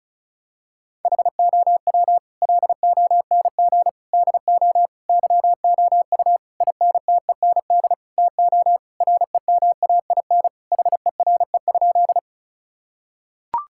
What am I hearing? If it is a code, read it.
Morse code, heard as a keyed tone, about 36 wpm, 700 Hz. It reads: HOW LONG DO YOU INTEND TO REMAIN HERE?
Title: Sentences from Top 1600 Words 35wpm